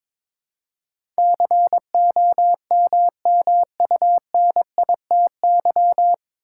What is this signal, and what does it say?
Morse code, heard as a keyed tone, about 22 wpm, 700 Hz. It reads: COMMUNITY
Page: Morse Code Ninja practice 22wpm